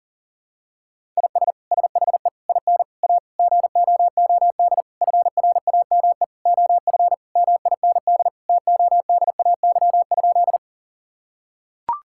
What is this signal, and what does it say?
Morse code, heard as a keyed tone, about 40 wpm, 700 Hz. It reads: IS SHE IN A GOOD FRAME OF MIND TODAY?
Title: Sentences from Top 1000 Words - Review Part 6 - 40wpm